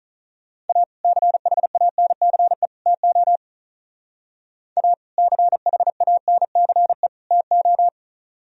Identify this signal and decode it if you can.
Morse code, heard as a keyed tone, about 45 wpm, 700 Hz. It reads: A CHANCE TO A CHANCE TO